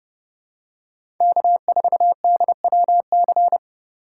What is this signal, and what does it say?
Morse code, heard as a keyed tone, about 30 wpm, 700 Hz. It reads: K4DWC
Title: USA Call Signs - 3x Character Spacing - 30wpm